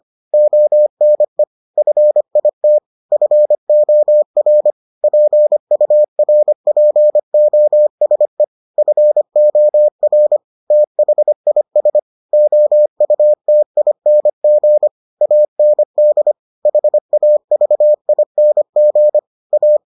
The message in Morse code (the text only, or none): ONE FIT FOR PURPOSE FOR THIS OUTING AND HAVING A